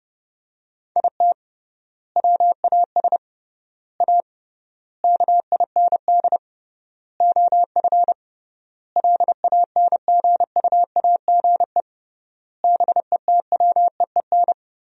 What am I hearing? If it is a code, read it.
Morse code, heard as a keyed tone, about 30 wpm, 700 Hz. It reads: IT WAS A KIND OF LANGUAGE BETWEEN
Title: 3x Word-Spacing - Sentences from Top 500 Words - 30wpm